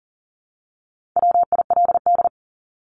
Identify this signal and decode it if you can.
Morse code, heard as a keyed tone, about 40 wpm, 700 Hz. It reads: WILD
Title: Top 1501-1750 Words 40wpm